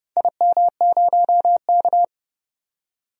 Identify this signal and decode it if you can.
Morse code, heard as a keyed tone, about 32 wpm, 700 Hz. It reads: IM0K
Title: DXCC-100 Call Signs - 4 Characters - 30wpm